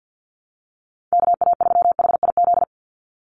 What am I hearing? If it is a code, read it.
Morse code, heard as a keyed tone, about 50 wpm, 700 Hz. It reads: KA3HIZ
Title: USA Call Signs 50wpm